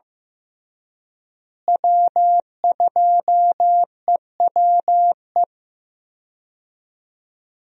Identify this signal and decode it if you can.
Morse code, heard as a keyed tone, about 14 wpm, 700 Hz. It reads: W2EWE
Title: USA Call Signs - 5 Characters - Repeated 3x - 15wpm